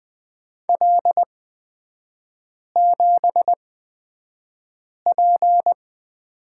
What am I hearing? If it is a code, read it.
Morse code, heard as a keyed tone, about 20 wpm, spaced as for 5 wpm, 700 Hz. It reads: L7P
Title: 224 Lesson 36e - BT - Three Characters 20wpm